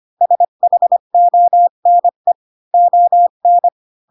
Morse code, heard as a keyed tone, about 25 wpm, 700 Hz. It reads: SHONE ON
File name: Pinocchio / Pinocchio ch1 20 25 ik1hge